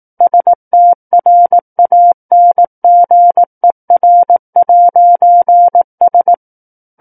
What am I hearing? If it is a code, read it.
Morse code, heard as a keyed tone, about 18 wpm, 700 Hz. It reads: STRANGER'S
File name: 18 WPM